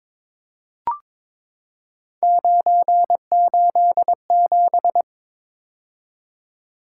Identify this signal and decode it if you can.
Morse code, heard as a keyed tone, about 22 wpm, 700 Hz. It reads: 987
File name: practice / 22wpm / Three Letter-Number Combinations - 22wpm